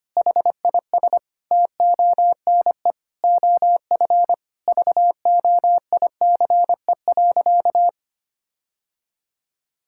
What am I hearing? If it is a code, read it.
Morse code, heard as a keyed tone, about 25 wpm, 700 Hz. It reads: HIS TONE OF VOICE.
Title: Sentences from Top 1600 Words 25wpm